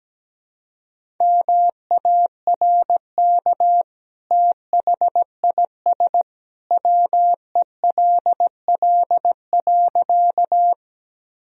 Morse code, heard as a keyed tone, about 17 wpm, 700 Hz. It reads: MARK THIS WELL.
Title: Sentences from Top 1000 Words - Review Part 1 - 17wpm